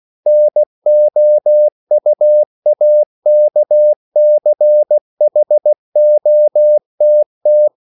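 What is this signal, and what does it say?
Morse code, heard as a keyed tone, about 16 wpm, 600 Hz. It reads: NOUAKCHOTT
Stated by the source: CW practice of national capitals at 16 words per minute, from ON6ZQ.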